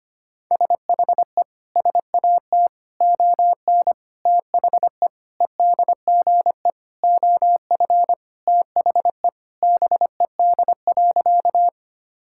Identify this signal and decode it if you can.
Morse code, heard as a keyed tone, about 25 wpm, 700 Hz. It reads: SHE SAT ON THE EDGE OF THE BED.
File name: practice / 25wpm / Sentences with 4-Word Phrases - 25wpm